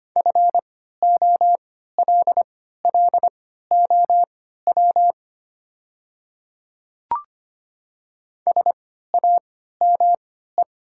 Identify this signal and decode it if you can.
Morse code, heard as a keyed tone, about 25 wpm, spaced as for 14 wpm, 700 Hz. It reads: FOLLOW SAME
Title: 3x Character Spacing - Top 200 Words - 25wpm